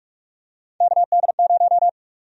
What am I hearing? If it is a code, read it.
Morse code, heard as a keyed tone, about 48 wpm, 700 Hz. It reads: KD0
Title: USA Call Sign Prefixes - 45wpm